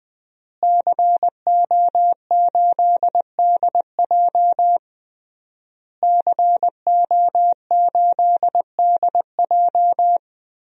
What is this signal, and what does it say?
Morse code, heard as a keyed tone, about 20 wpm, 700 Hz. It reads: CO8DJ CO8DJ
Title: DXCC-100 Call Signs - 5 Characters - Repeated 3x - 20wpm